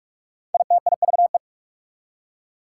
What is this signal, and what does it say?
Morse code, heard as a keyed tone, about 55 wpm, 700 Hz. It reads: ITIVE
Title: Speed-Racing Prefixes and Suffixes 45wpm